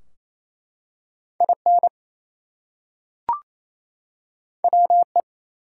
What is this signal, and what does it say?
Morse code, heard as a keyed tone, about 27 wpm, 700 Hz. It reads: IN WE
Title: Two Letter Words - 28wpm